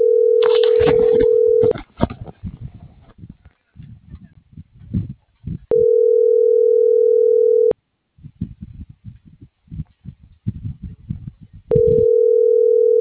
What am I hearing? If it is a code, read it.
Ringback tone (North America), 440 + 480 Hz.